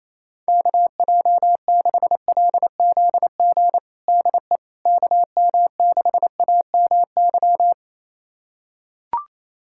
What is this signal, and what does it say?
Morse code, heard as a keyed tone, about 28 wpm, 700 Hz. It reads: KJ6LZG DE KM6AMY
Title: USA Callsign DE Callsign - 6 Characters - 28wpm